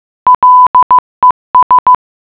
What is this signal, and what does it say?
Morse code, heard as a keyed tone, about 15 wpm, 1000 Hz. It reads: LES